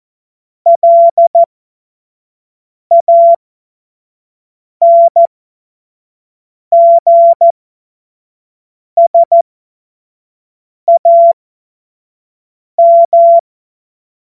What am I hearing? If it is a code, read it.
Morse code, heard as a keyed tone, about 14 wpm, spaced as for 5 wpm, 680 Hz. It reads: LANGSAM
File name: Morsekurs / Witze / Witz 29. November 2022 25 F70